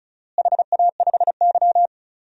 Morse code, heard as a keyed tone, about 36 wpm, 700 Hz. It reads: HA5Y
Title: DXCC-100 Call Signs - 4 Characters - Repeated 3x - 35wpm